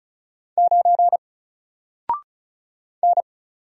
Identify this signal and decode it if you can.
Morse code, heard as a keyed tone, about 36 wpm, 700 Hz. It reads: 9 N
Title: Single Character (Letter:Number:Punctuation:Prosign) 35wpm